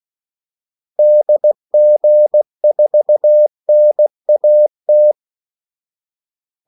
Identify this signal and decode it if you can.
Morse code, heard as a keyed tone, about 16 wpm, 600 Hz. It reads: DG4NAT
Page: Kaldesignaler 16 wpm